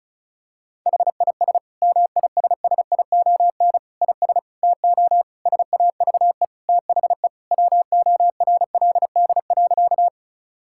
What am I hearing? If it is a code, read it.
Morse code, heard as a keyed tone, about 36 wpm, 700 Hz. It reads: HIS MISSION IS TO SAVE THE WORLD.
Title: Sentences from Top 1600 Words 35wpm